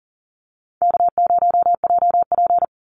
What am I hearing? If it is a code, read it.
Morse code, heard as a keyed tone, about 39 wpm, 700 Hz. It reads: K0JP
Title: Call Signs Repeated 3x - 40wpm